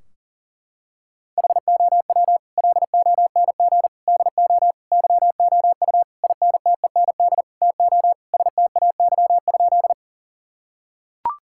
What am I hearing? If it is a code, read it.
Morse code, heard as a keyed tone, about 40 wpm, 700 Hz. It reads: HOW LONG DO YOU INTEND TO STAY?
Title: Sentences from Top 1600 Words 40wpm